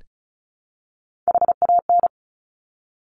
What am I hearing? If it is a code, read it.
Morse code, heard as a keyed tone, about 36 wpm, 700 Hz. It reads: HAN
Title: Top Three Letter Combinations in Words 35wpm